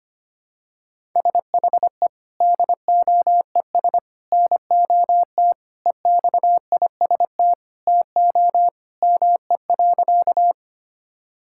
Morse code, heard as a keyed tone, about 25 wpm, 700 Hz. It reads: SHE DOES NOT EXIST TO ME.
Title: Sentences from Top 1000 Words - Review Part 5 - 25wpm